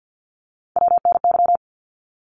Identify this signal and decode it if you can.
Morse code, heard as a keyed tone, about 50 wpm, 700 Hz. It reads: WNY
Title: ARRL Sections 50wpm